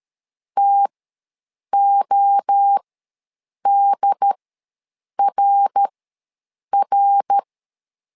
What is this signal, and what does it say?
Morse code, heard as a keyed tone, about 13 wpm, spaced as for 7 wpm, 800 Hz. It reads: TODRR
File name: LEZIONI / 3 MWRO / 3 MWRO ws 12wpm 1914